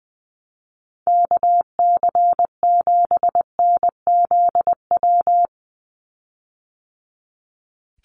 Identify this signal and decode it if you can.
Morse code, heard as a keyed tone, about 20 wpm, 700 Hz. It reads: KC7NZW
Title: USA Call Signs 20wpm